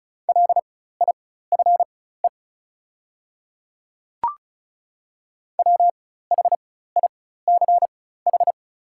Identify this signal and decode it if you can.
Morse code, heard as a keyed tone, about 36 wpm, spaced as for 16 wpm, 690 Hz. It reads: LIFE WHICH
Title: 4x Character Spacing - Top 300 Words - 35wpm